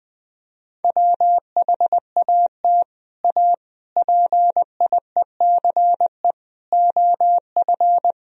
Morse code, heard as a keyed tone, about 20 wpm, 700 Hz. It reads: WHAT A PIECE OF
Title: Sentences from Top 1000 Words - Review Part 2 - 20wpm